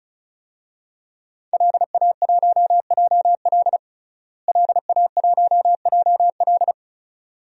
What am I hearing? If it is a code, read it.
Morse code, heard as a keyed tone, about 34 wpm, 700 Hz. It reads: LA1JL LA1JL
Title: DXCC-100 Call Signs - 5 Characters - Repeated 2x - 35wpm